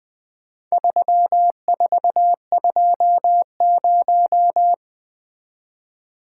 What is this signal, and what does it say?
Morse code, heard as a keyed tone, about 20 wpm, 700 Hz. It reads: 3420
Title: Four Letter-Number Combinations - 20wpm